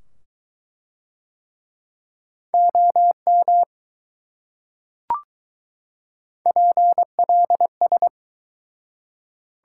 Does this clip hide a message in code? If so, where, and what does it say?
Yes: Morse code from 2.5 to 8.1 s, heard as a keyed tone, about 20 wpm, 700 Hz. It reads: OM PLS